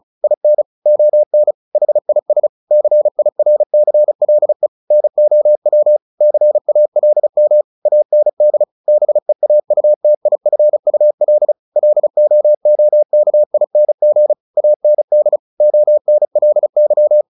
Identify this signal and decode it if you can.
Morse code, heard as a keyed tone, about 34 wpm, 600 Hz. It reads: IN ON HIS CIRCLE NOW CALM AND BEAUTIFUL LOOKING AND ONLY
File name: the old man and the sea 2 35wpm